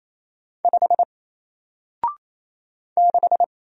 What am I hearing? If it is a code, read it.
Morse code, heard as a keyed tone, about 27 wpm, 700 Hz. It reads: 5 6
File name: Single-Letter-Number-28wpm